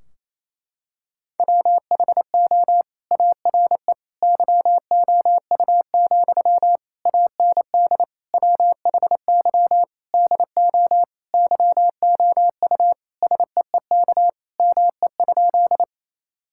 Morse code, heard as a keyed tone, about 28 wpm, 700 Hz. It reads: WHO ARE YOU, AND WHY DO YOU SEEK ME?
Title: Sentences from Top 1600 Words 28wpm